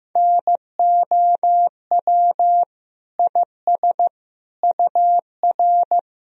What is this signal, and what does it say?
Morse code, heard as a keyed tone, about 15 wpm, 700 Hz. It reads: NOW IS UR
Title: Sentences from Top 100 Words and Abbreviations 15wpm